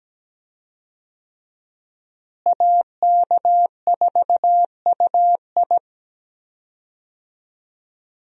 Morse code, heard as a keyed tone, about 17 wpm, 700 Hz. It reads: AK4UI
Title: USA Call Signs - 5 Characters - Repeated 3x - 17wpm